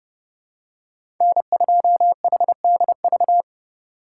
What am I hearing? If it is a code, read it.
Morse code, heard as a keyed tone, about 30 wpm, 700 Hz. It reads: N2HDV
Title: USA Call Signs - 5 Characters - 4x Character Spacing - 30wpm